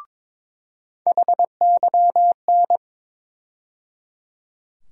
Morse code, heard as a keyed tone, about 22 wpm, 700 Hz. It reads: HYN
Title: USA Call Sign Postfixes - 3 Letters 22wpm